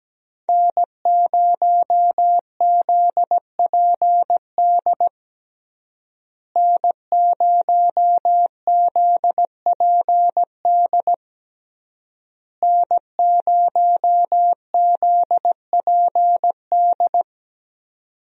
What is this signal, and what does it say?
Morse code, heard as a keyed tone, about 17 wpm, 700 Hz. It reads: N0ZPD N0ZPD N0ZPD
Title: USA Call Signs - 5 Characters - Repeated 3x - 17wpm